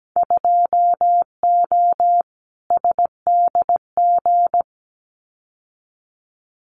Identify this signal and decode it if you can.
Morse code, heard as a keyed tone, about 17 wpm, 700 Hz. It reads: 2O SDG
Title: Winter Field Day Exchanges 17wpm